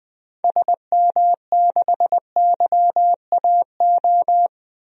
Morse code, heard as a keyed tone, about 20 wpm, 700 Hz. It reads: SM6YAO
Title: DXCC-100 Call Signs - 6 Characters - 20wpm